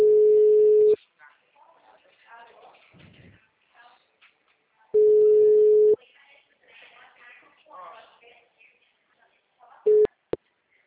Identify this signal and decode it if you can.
Ringback tone (Europe, 425 Hz), with speech.
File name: out- -2003- - - .32677